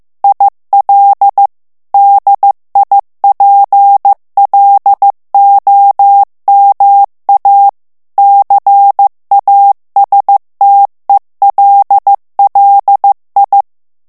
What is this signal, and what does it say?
Morse code, heard as a keyed tone, about 15 wpm, 800 Hz. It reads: IL DIPLOMA CASTELLI
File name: ITALIEN texte clair 1 vitesse 14 mots minute